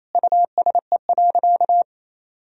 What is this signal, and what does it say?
Morse code, heard as a keyed tone, about 28 wpm, 700 Hz. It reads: USE.
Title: Sentences from Top 100 Words and Abbreviations 28wpm